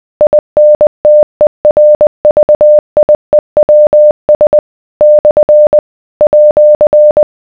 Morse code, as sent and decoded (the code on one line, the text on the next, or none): .. -. - . .-. ...- .. .-- ... / -..-. / .--.-.
INTERVIEWS / @